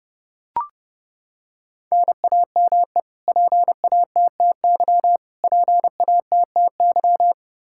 Morse code, heard as a keyed tone, about 30 wpm, 700 Hz. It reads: NAME PATTY PATTY
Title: Practice QSO - Exchange 1 - Beginning - 30wpm